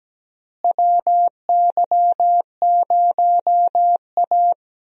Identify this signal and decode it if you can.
Morse code, heard as a keyed tone, about 17 wpm, 700 Hz. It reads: WY0A
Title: USA Call Signs - 4 Characters - Repeated 2x - 17wpm